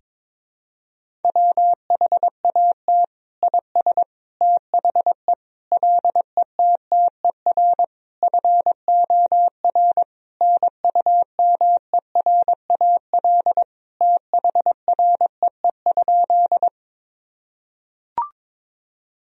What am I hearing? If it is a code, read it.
Morse code, heard as a keyed tone, about 22 wpm, 700 Hz. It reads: WHAT IS THE LETTER FOR NUMERAL THREE?
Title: Sentences from Top 1000 Words - Review Part 2 - 22wpm